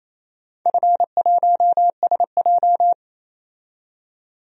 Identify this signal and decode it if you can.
Morse code, heard as a keyed tone, about 28 wpm, 700 Hz. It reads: F1SJ
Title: DXCC-100 Call Signs - 4 Characters - 28wpm